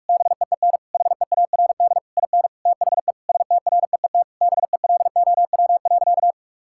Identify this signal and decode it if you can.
Morse code, heard as a keyed tone, about 46 wpm, 700 Hz. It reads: BEEN HEARD IN THE STREET BELOW.
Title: Sentences from Top 1000 Words - Review Part 5 - 45wpm